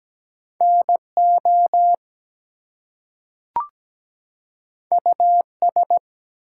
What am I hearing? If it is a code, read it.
Morse code, heard as a keyed tone, about 17 wpm, 700 Hz. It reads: NO US
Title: Two Letter Words - 17wpm